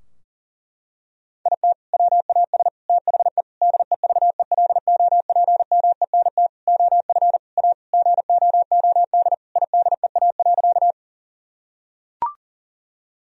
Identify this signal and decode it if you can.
Morse code, heard as a keyed tone, about 40 wpm, 700 Hz. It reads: IT WAS THE DEVELOPMENT OF A GOOD IDEA.
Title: Sentences from Top 1000 Words - Review Part 4 - 40wpm